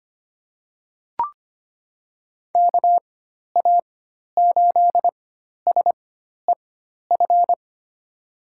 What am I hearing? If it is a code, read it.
Morse code, heard as a keyed tone, about 25 wpm, spaced as for 12 wpm, 700 Hz. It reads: KA8SEF
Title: USA Call Signs - 6 Characters - 4x Character Spacing - 25wpm